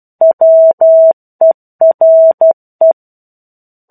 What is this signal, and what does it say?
Morse code, heard as a keyed tone, about 12 wpm, 650 Hz. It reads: WERE